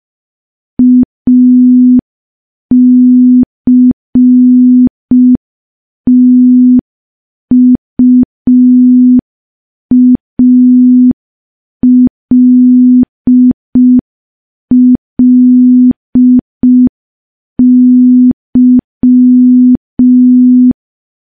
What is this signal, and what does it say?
Morse code, heard as a keyed tone, about 5 wpm, 250 Hz. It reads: ACTUALLY
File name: morse / morse-training-05WPM